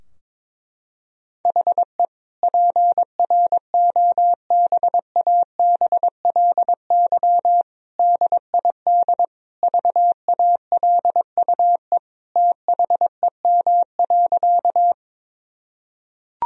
Morse code, heard as a keyed tone, about 21 wpm, 700 Hz. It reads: HE PROBABLY DID VALUE THEM.
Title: Sentences from Top 1000 Words - Review Part 4 - 22wpm